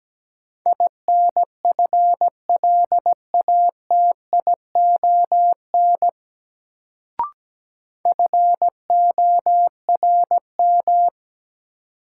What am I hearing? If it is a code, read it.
Morse code, heard as a keyed tone, about 17 wpm, 700 Hz. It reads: INFLATION FORM
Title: Sentences from Top 1100 Words - New Word Drill Alt - 17wpm